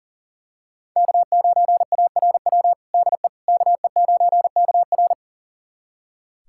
Morse code, heard as a keyed tone, about 40 wpm, 700 Hz. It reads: K9ARW DE KE9KR
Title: USA Callsign DE Callsign - 5 Characters - 40wpm